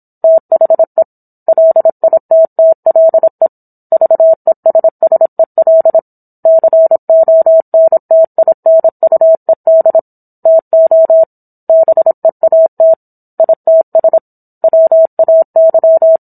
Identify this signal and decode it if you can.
Morse code, heard as a keyed tone, about 27 wpm, 650 Hz. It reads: THE LITTLE VESSEL CONTINUED TO BEAT ITS WAY